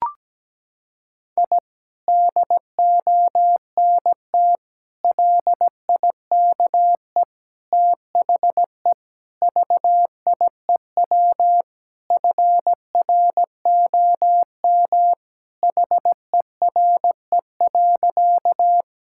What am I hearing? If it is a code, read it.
Morse code, heard as a keyed tone, about 17 wpm, 700 Hz. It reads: I DONT LIKE THE VIEW FROM HERE.
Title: Sentences from Top 1000 Words - Review Part 6 - 17wpm